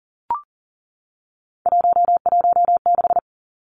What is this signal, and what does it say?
Morse code, heard as a keyed tone, about 40 wpm, 700 Hz. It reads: 116
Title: Numbers - 3-Digit 40wpm